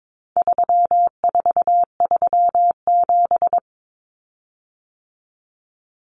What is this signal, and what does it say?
Morse code, heard as a keyed tone, about 22 wpm, 700 Hz. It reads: 3437
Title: Common Cut Numbers - 4-Digit 22wpm